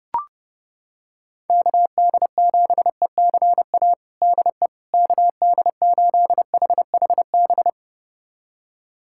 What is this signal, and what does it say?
Morse code, heard as a keyed tone, about 30 wpm, 700 Hz. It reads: KD7ECA DE KD8HHB
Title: USA Callsign DE Callsign - 6 Characters - 30wpm